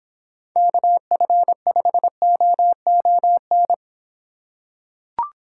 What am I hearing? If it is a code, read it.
Morse code, heard as a keyed tone, about 26 wpm, 700 Hz. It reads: KF5OON